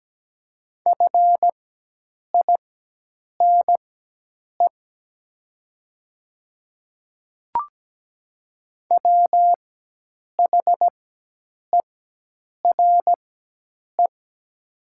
Morse code, heard as a keyed tone, about 17 wpm, spaced as for 8 wpm, 700 Hz. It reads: FINE WHERE
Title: 4x Character Spacing - Top 500 Words - 17wpm